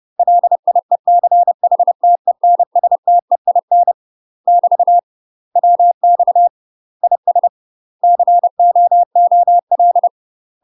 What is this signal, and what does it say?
Morse code, heard as a keyed tone, about 30 wpm, 700 Hz. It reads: LIECHTENSTEIN <BT> WX IS COOL
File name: UK MASTER QSOs TRAINING / 30WPM / UK MASTER QSOs 0038